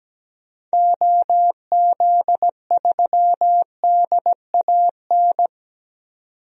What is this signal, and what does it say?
Morse code, heard as a keyed tone, about 17 wpm, 700 Hz. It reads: OZ3DAN
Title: DXCC-100 Call Signs - 6 Characters - 17wpm